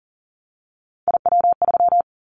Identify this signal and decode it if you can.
Morse code, heard as a keyed tone, about 40 wpm, 700 Hz. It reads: IW3
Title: Callsign Prefixes for Top 25 DXCC - Repeated 3x - 40wpm